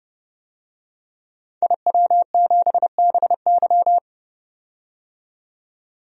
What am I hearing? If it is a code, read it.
Morse code, heard as a keyed tone, about 30 wpm, 700 Hz. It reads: IW7BY
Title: DXCC-100 Call Signs - 5 Characters - 30wpm